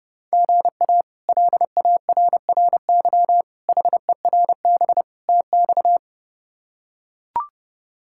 Morse code, heard as a keyed tone, about 32 wpm, 700 Hz. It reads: GA LARRY HERB TX
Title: K1USN SST QSO Practice - 30wpm